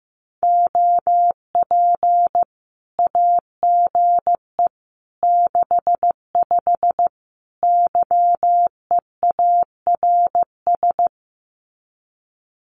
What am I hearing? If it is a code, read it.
Morse code, heard as a keyed tone, about 15 wpm, 700 Hz. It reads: OP AGE 65 YEARS
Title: Common QSO Phrases 15wpm